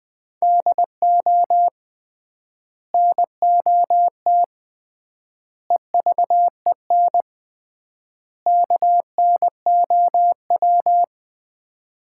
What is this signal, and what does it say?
Morse code, heard as a keyed tone, about 20 wpm, 700 Hz. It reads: DO NOT EVEN KNOW
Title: 3x Word-Spacing - Sets of 4 Words - 20wpm